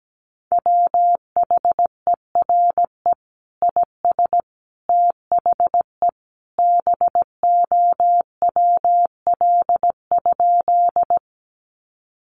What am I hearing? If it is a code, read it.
Morse code, heard as a keyed tone, about 17 wpm, 700 Hz. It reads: WHERE IS THE BOWL?
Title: Sentences from Top 1200 Words 17wpm